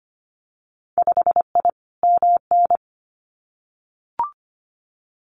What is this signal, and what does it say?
Morse code, heard as a keyed tone, about 25 wpm, 690 Hz. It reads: 5I MN
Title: Winter Field Day Exchanges Encore 25wpm